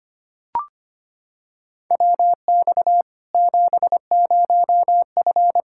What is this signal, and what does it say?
Morse code, heard as a keyed tone, about 25 wpm, 700 Hz. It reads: WX 70F
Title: Practice QSO - Exchange 2 - Info - 25wpm